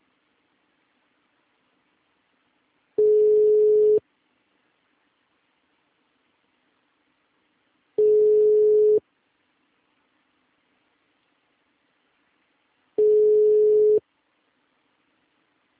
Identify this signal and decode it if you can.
Ringback tone (Europe, 425 Hz).